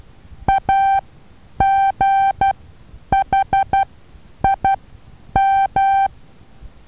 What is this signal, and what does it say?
Morse code, heard as a keyed tone, about 12 wpm, spaced as for 9 wpm, 780 Hz. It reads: AGHIM
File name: Taviro HA6VA B